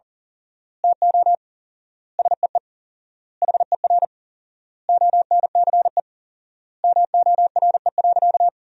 Morse code, heard as a keyed tone, about 40 wpm, 700 Hz. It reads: TO SEE HER ONCE MORE.
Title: 4x Word-Spacing - Sentences from Top 500 Words - 40wpm